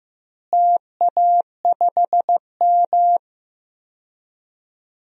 Morse code, heard as a keyed tone, about 15 wpm, 700 Hz. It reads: TA5M
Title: DXCC-100 Call Signs - 4 Characters - Repeated 3x - 15wpm